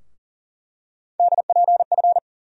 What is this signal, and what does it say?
Morse code, heard as a keyed tone, about 40 wpm, 700 Hz. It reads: DPF
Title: USA Call Sign Postfixes - 3 Letters 40wpm